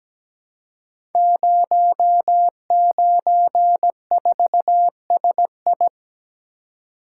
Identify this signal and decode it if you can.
Morse code, heard as a keyed tone, about 17 wpm, 700 Hz. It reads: 094SI
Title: Code Groups - 5 Letter:Numbers - 17wpm